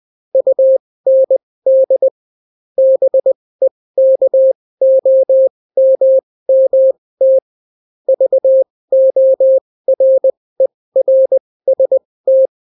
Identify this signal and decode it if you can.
Morse code, heard as a keyed tone, about 20 wpm, spaced as for 16 wpm, 530 Hz. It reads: UND BEKOMMT VORERST